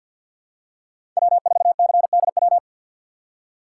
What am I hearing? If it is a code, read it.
Morse code, heard as a keyed tone, about 52 wpm, 700 Hz. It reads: W4CDW
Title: USA Call Signs - 5 Characters - 6x Character Spacing - 50wpm